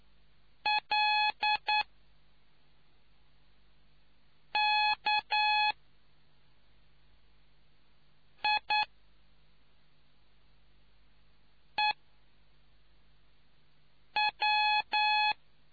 Morse code, heard as a keyed tone, about 9 wpm, spaced as for 3 wpm, 850 Hz. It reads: LKIEW